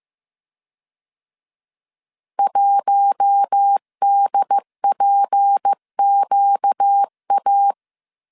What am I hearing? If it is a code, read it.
Morse code, heard as a keyed tone, about 15 wpm, 800 Hz. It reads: 1DPQA